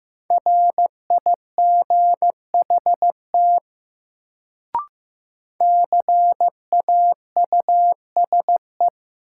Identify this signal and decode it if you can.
Morse code, heard as a keyed tone, about 15 wpm, 700 Hz. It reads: RIGHT CAUSE
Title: Sentences from Top 200 Words - New Word Drill - 15wpm